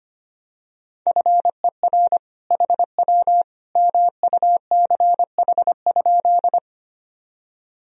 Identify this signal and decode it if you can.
Morse code, heard as a keyed tone, about 25 wpm, 700 Hz. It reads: FER HW MUCH?
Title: Sentences from Top 200 Words and Abbreviations 25wpm